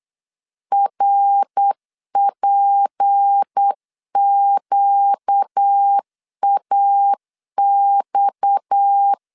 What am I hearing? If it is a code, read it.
Morse code, heard as a keyed tone, about 8 wpm, 800 Hz. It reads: RPQAX
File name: 8wpm